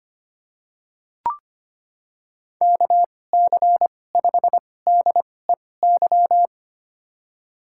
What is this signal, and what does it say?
Morse code, heard as a keyed tone, about 25 wpm, spaced as for 18 wpm, 700 Hz. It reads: KC5DEY
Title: USA Call Signs - 6 Characters - 2x Character Spacing - 25wpm